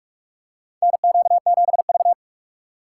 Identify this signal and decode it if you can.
Morse code, heard as a keyed tone, about 44 wpm, 700 Hz. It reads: NQ7V
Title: USA Call Signs - 6x Character Spacing - 45wpm